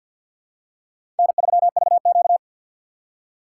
Morse code, heard as a keyed tone, about 52 wpm, 700 Hz. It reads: N3VQ
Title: USA Call Signs - 4 Characters - 2x Character Spacing - 50wpm